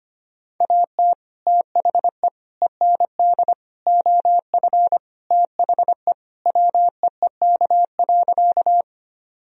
Morse code, heard as a keyed tone, about 25 wpm, 700 Hz. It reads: AT THE END OF THE WEEK.
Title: Sentences from Top 1000 Words - Review Part 2 - 25wpm